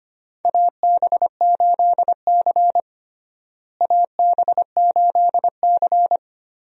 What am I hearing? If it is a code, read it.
Morse code, heard as a keyed tone, about 25 wpm, 700 Hz. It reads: AB8C AB8C
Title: USA Call Signs - 4 Characters - Repeated 2x - 25wpm